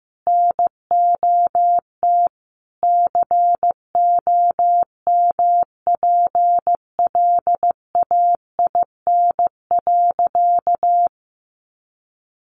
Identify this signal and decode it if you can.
Morse code, heard as a keyed tone, about 15 wpm, 700 Hz. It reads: NOT COMPLAIN.
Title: Sentences from Top 1300 Words 15wpm